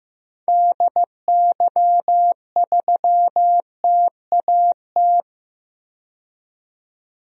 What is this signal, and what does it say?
Morse code, heard as a keyed tone, about 15 wpm, 700 Hz. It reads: DY3TAT
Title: DXCC-100 Call Signs - 6 Characters - 15wpm